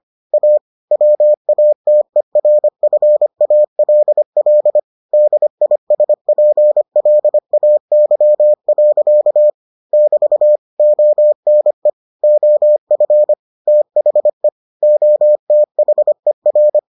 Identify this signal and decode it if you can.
Morse code, heard as a keyed tone, about 25 wpm, 600 Hz. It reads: A WATERFALL DISPLAY. <BT> ONE OF THE OTHER